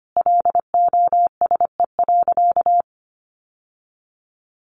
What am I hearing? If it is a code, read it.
Morse code, heard as a keyed tone, about 25 wpm, 700 Hz. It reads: LOSE.
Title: Sentences from Top 1100 Words 25wpm